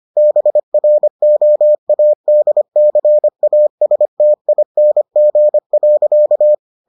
Morse code, heard as a keyed tone, about 25 wpm, 600 Hz. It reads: BROADCASTING.